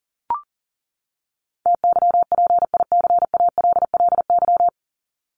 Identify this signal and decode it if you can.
Morse code, heard as a keyed tone, about 40 wpm, 700 Hz. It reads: TYPICALLY
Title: LY Suffix - Top 100 Words with Base Words 40wpm